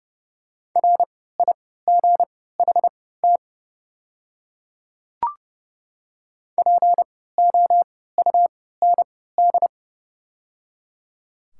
Morse code, heard as a keyed tone, about 29 wpm, spaced as for 17 wpm, 700 Hz. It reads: RIGHT POUND
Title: 3x Character Spacing - Top 500 Words - 30wpm